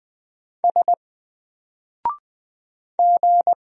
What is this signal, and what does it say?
Morse code, heard as a keyed tone, about 20 wpm, 700 Hz. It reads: S G
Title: Single Character (Letter:Number:Punctuation:Prosign) Alt 20wpm